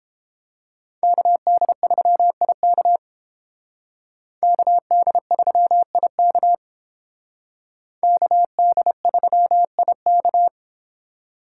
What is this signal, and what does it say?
Morse code, heard as a keyed tone, about 29 wpm, 700 Hz. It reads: KD3IK KD3IK KD3IK